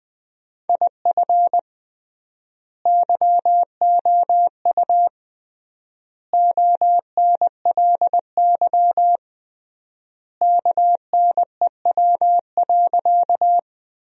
Morse code, heard as a keyed tone, about 20 wpm, 700 Hz. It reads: IF YOU ONLY KNEW.
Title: 3x Word-Spacing - Sentences from Top 400 Words - 20wpm